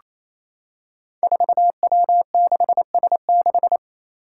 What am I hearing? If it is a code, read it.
Morse code, heard as a keyed tone, about 27 wpm, 700 Hz. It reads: 4W6S6
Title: Five Challenging Letter-Number Combinations 28wpm